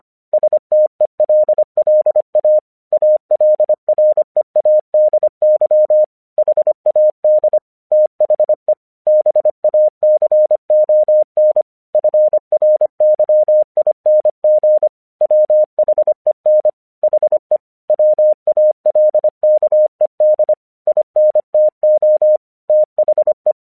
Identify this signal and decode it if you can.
Morse code, heard as a keyed tone, about 25 wpm, 610 Hz. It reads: STELLA ALREADY HAD THE BACON FRYING WHEN HE WALKED INTO THE
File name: English 25wpm